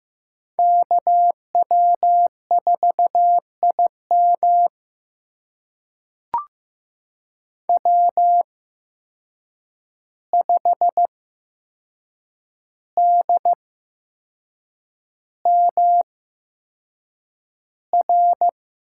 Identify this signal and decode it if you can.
Morse code, heard as a keyed tone, about 15 wpm, spaced as for 6 wpm, 700 Hz. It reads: KW4IM W5DMR
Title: USA Call Signs - 5 Characters - 8x Character Spacing - 15wpm